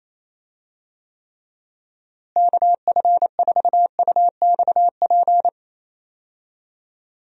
Morse code, heard as a keyed tone, about 28 wpm, 700 Hz. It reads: KF4UXP
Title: USA Call Signs - 6 Characters - Repeated 3x - 28wpm